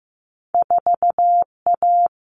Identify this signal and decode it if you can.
Morse code, heard as a keyed tone, about 15 wpm, 700 Hz. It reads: 4A
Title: Common Cut Numbers - 2-Digit 15wpm